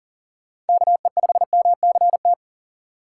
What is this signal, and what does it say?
Morse code, heard as a keyed tone, about 40 wpm, 700 Hz. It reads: KE5MCT
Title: USA Call Signs - 6 Characters - 6x Character Spacing - 40wpm